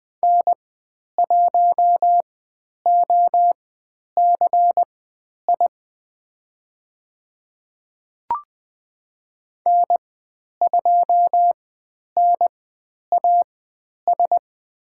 Morse code, heard as a keyed tone, about 20 wpm, spaced as for 10 wpm, 700 Hz. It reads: N1OCI N2NAS
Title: 095 Lesson 16g - Letter c - Call Signs 20wpm